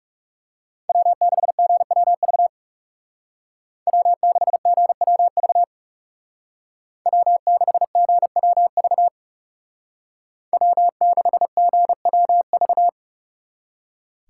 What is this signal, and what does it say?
Morse code, heard as a keyed tone, about 39 wpm, 700 Hz. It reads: W6GWV W6GWV W6GWV W6GWV